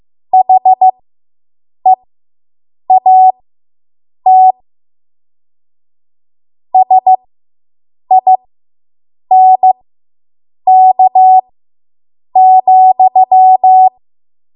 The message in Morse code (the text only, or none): HEAT SINK,